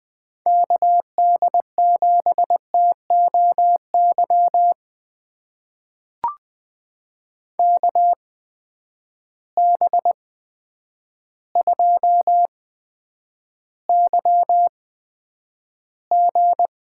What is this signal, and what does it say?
Morse code, heard as a keyed tone, about 20 wpm, spaced as for 9 wpm, 700 Hz. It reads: KD7TOY KB2YG
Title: USA Call Signs - 6 Characters - 8x Character Spacing - 20wpm